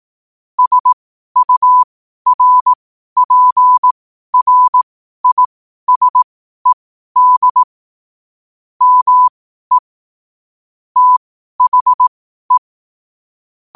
Morse code, heard as a keyed tone, about 18 wpm, spaced as for 12 wpm, 1000 Hz. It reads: SURPRISED ME THE
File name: qotdfiles / 12wpm